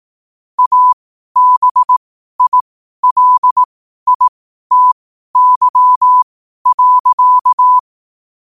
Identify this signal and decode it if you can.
Morse code, heard as a keyed tone, about 18 wpm, spaced as for 12 wpm, 1000 Hz. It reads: ABILITY.
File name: qotdfiles / 12wpm